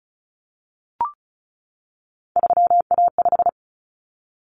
Morse code, heard as a keyed tone, about 34 wpm, 700 Hz. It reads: 3A5
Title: Common Cut Numbers - 3-Digit 35wpm